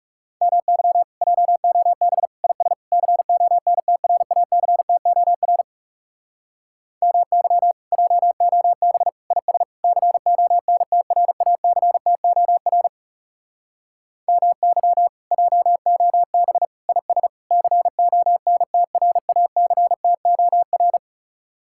Morse code, heard as a keyed tone, about 37 wpm, 700 Hz. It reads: MY JOB IS CONTRACTOR MY JOB IS CONTRACTOR MY JOB IS CONTRACTOR
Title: Speed-Racing Common QSO Phrases - 30wpm